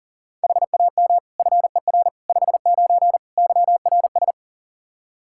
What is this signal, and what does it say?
Morse code, heard as a keyed tone, about 40 wpm, 700 Hz. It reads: HAM FER 59 YRS
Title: Practice QSO - Exchange 3 - More Info - 40wpm